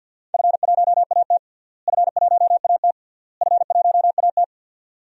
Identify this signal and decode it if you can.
Morse code, heard as a keyed tone, about 47 wpm, 700 Hz. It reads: F1AT F1AT F1AT
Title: DXCC-100 Call Signs - 4 Characters - Repeated 3x - 50wpm